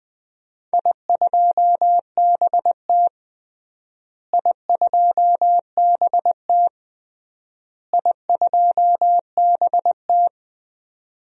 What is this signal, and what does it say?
Morse code, heard as a keyed tone, about 20 wpm, 700 Hz. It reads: I2BT I2BT I2BT
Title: DXCC-100 Call Signs - 4 Characters - Repeated 3x - 20wpm